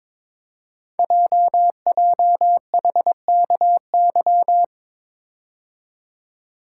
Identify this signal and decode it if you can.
Morse code, heard as a keyed tone, about 22 wpm, 700 Hz. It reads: JJHKY
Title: Code Groups - 5 Letters - 22wpm